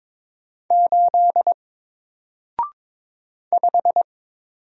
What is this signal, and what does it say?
Morse code, heard as a keyed tone, about 22 wpm, 700 Hz. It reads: 8 5
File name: Single-Letter-Number-22wpm